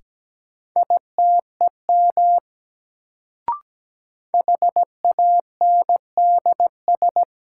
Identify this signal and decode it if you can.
Morse code, heard as a keyed tone, about 17 wpm, 700 Hz. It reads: ITEM HANDS
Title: Sentences from Top 1000 Words - Review Part 6 - New Word Drill - 17wpm